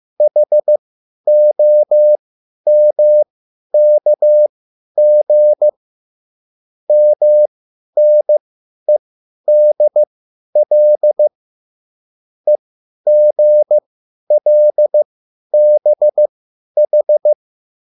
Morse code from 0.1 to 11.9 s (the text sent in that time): HOMKG MNEDL